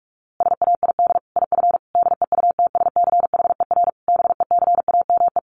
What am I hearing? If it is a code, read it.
Morse code, heard as a keyed tone, about 44 wpm, 700 Hz. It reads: SAID IF DEUTSCHER BECAME
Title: A Sound Of Thunder 45wpm